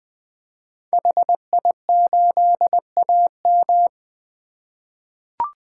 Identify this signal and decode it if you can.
Morse code, heard as a keyed tone, about 21 wpm, 700 Hz. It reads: HI8AM
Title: 2023 POTA Activator Call Signs - 20wpm